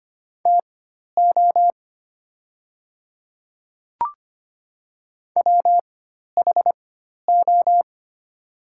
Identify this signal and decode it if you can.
Morse code, heard as a keyed tone, about 25 wpm, spaced as for 12 wpm, 700 Hz. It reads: TO WHO
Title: 4x Character Spacing - Top 100 Words - 25wpm